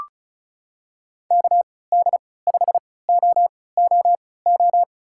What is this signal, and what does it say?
Morse code, heard as a keyed tone, about 36 wpm, spaced as for 20 wpm, 690 Hz. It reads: KD5OOO
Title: USA Call Signs - 6 Characters - 3x Character Spacing - 35wpm